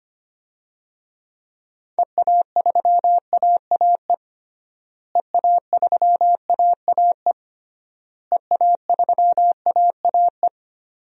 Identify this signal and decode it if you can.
Morse code, heard as a keyed tone, about 25 wpm, 700 Hz. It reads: EA3AAE EA3AAE EA3AAE
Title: DXCC-100 Call Signs - 6 Characters - Repeated 3x - 25wpm